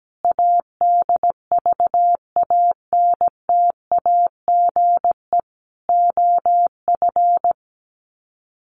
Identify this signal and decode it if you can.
Morse code, heard as a keyed tone, about 17 wpm, 700 Hz. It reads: ADVANTAGE OF
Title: Sets of 4 Words 17wpm